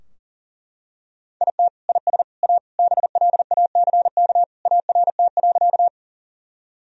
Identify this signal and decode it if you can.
Morse code, heard as a keyed tone, about 40 wpm, 700 Hz. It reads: IT IS A BLACK ART.
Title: Sentences from Top 1000 Words - Review Part 3 - 40wpm